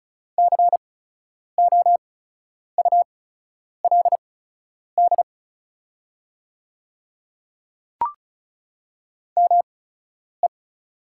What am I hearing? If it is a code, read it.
Morse code, heard as a keyed tone, about 34 wpm, spaced as for 10 wpm, 690 Hz. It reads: COULD ME
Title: 8x Character Spacing - Top 200 Words - 35wpm